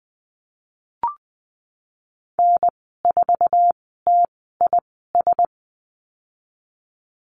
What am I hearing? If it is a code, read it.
Morse code, heard as a keyed tone, about 20 wpm, spaced as for 15 wpm, 700 Hz. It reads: N4TIS
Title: 048 Lesson 9g - Number 4 - Call Signs 20wpm